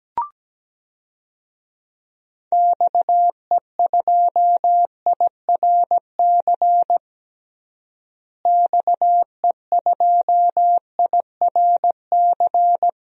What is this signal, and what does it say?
Morse code, heard as a keyed tone, about 17 wpm, 700 Hz. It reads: XE2IRC XE2IRC
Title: DXCC-100 Call Signs - 6 Characters - Repeated 3x - 17wpm